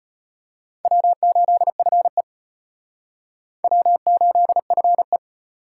Morse code, heard as a keyed tone, about 31 wpm, 700 Hz. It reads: W8FE W8FE
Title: Speed Racing - USA Call Signs 25wpm